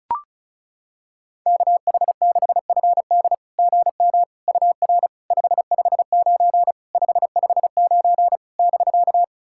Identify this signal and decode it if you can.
Morse code, heard as a keyed tone, about 36 wpm, 690 Hz. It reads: KH6FD GM UR 559 559 <BK>
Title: SOTA QSO Practice 35wpm